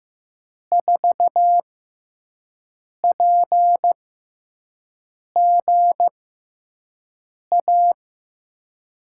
Morse code, heard as a keyed tone, about 15 wpm, spaced as for 5 wpm, 700 Hz. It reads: 4PGA
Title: USA Call Signs - 6 Characters - 6x Character Spacing - 15wpm